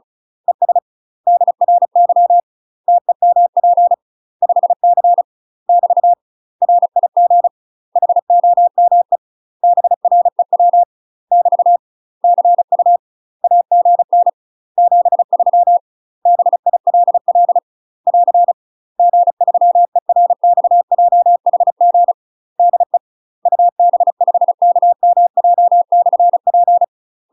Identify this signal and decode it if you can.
Morse code, heard as a keyed tone, about 36 wpm, 700 Hz. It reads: ES DRY TEMP 5C <BT> RIG HOME BREW <BT> CU AGN 73 BILL <AR> G3ERXJHG DE UB5KMJ/P
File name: UK MASTER QSOs TRAINING / 35WPM / UK MASTER QSOs 0014